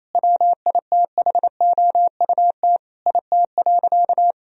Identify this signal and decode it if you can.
Morse code, heard as a keyed tone, about 28 wpm, 700 Hz. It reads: WITHOUT IT.
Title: Sentences from Top 1000 Words - Review Part 3 - 28wpm